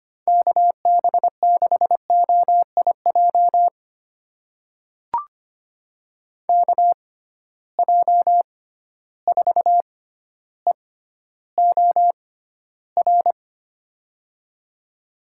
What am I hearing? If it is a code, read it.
Morse code, heard as a keyed tone, about 25 wpm, spaced as for 13 wpm, 700 Hz. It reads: KB6OIJ KJ4EOR
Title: USA Call Signs - 6 Characters - 6x Character Spacing - 25wpm